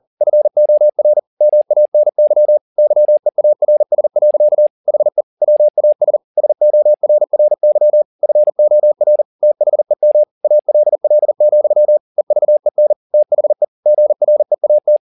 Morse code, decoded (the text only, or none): FOR MANY YEARS. HE WAS SORRY FOR THEM ALL, EVEN THE GREAT